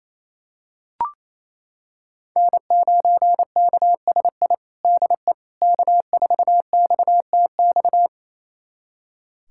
Morse code, heard as a keyed tone, about 28 wpm, 700 Hz. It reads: N9KSI DE K4XTX
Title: USA Callsign DE Callsign - 5 Characters - 28wpm